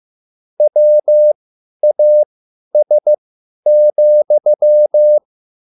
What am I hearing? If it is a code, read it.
Morse code, heard as a keyed tone, about 15 wpm, spaced as for 10 wpm, 600 Hz. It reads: WAS,